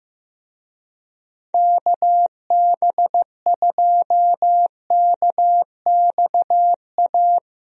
Morse code, heard as a keyed tone, about 15 wpm, 700 Hz. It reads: KB2KXA
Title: USA Call Signs - 6 Characters - Repeated 3x - 15wpm